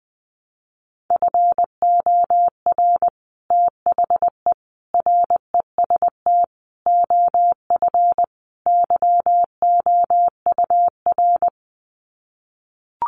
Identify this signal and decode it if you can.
Morse code, heard as a keyed tone, about 20 wpm, 700 Hz. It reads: FOR THE REST OF YOUR
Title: Sets of 5 Words Encore 20wpm